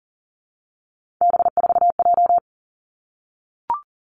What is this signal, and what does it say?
Morse code, heard as a keyed tone, about 42 wpm, 700 Hz. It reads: B4J
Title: Three Challenging Letter-Number Combinations 40wpm